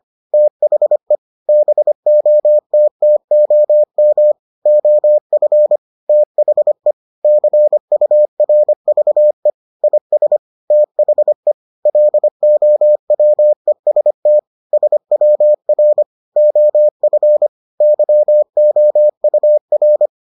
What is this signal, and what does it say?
Morse code, heard as a keyed tone, about 25 wpm, 600 Hz. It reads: THE BOTTOM OF THE CURVE IS THE LOWEST SWR OF YOUR